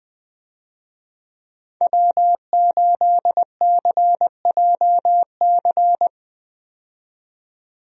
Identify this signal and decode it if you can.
Morse code, heard as a keyed tone, about 20 wpm, 700 Hz. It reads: W8CJC
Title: USA Call Signs - 5 Characters - 20wpm